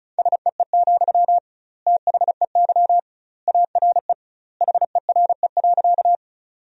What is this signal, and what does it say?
Morse code, heard as a keyed tone, about 36 wpm, 700 Hz. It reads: SEE, THEY ARE HERE.
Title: 2x Word-Spacing - Sentences from Top 200 Words - 35wpm